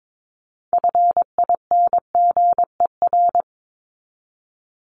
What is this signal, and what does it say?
Morse code, heard as a keyed tone, about 22 wpm, 700 Hz. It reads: FINGER
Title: Top 1501-1750 Words 22wpm